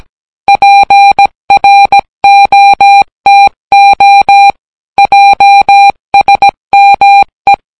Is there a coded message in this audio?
Yes: Morse code from 0.5 to 7.5 s, heard as a keyed tone, about 17 wpm, 800 Hz. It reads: PROTO JSME